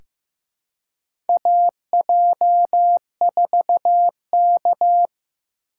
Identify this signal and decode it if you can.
Morse code, heard as a keyed tone, about 16 wpm, 700 Hz. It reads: AJ4K
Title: USA Call Signs - 4 Characters - 2x Character Spacing - 15wpm